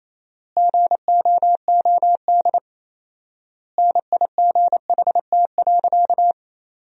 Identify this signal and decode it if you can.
Morse code, heard as a keyed tone, about 27 wpm, 700 Hz. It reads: GOOD NIGHT.
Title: 4x Word-Spacing - Sentences from Top 300 Words - 28wpm